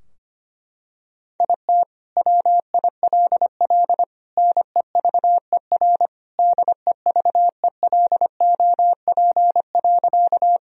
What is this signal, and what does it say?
Morse code, heard as a keyed tone, about 25 wpm, 700 Hz. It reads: IT WILL NEVER DEVELOP.
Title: Sentences from Top 1000 Words - Review Part 2 - 25wpm